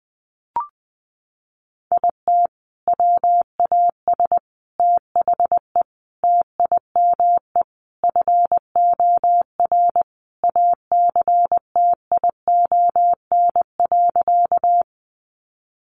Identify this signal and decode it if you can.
Morse code, heard as a keyed tone, about 20 wpm, 700 Hz. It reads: IT WAS THE TIME FOR ACTION.
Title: Sentences from Top 700 Words 20wpm